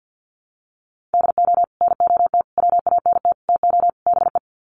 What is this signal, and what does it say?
Morse code, heard as a keyed tone, about 53 wpm, 700 Hz. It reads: DO NOT WANT TO BE
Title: Sets of 5 Words 50wpm